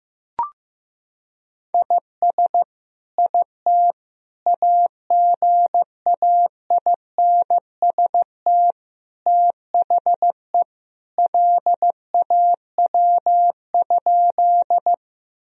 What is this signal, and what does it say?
Morse code, heard as a keyed tone, about 14 wpm, 700 Hz. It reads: IS IT AGAINST THE LAW?
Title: Sentences from Top 1000 Words - Review Part 3 - 15wpm